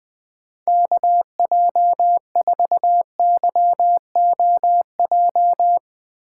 Morse code, heard as a keyed tone, about 20 wpm, 700 Hz. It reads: KJ4YOJ
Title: USA Call Signs - 6 Characters - 4x Character Spacing - 20wpm